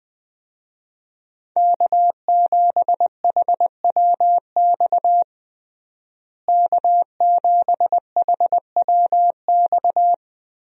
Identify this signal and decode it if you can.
Morse code, heard as a keyed tone, about 21 wpm, 700 Hz. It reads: K7HWX K7HWX
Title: USA Call Signs - 5 Characters - Repeated 3x - 20wpm